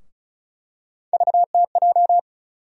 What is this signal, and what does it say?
Morse code, heard as a keyed tone, about 34 wpm, 700 Hz. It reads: VTJ
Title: USA Call Sign Postfixes - 3 Letters 35wpm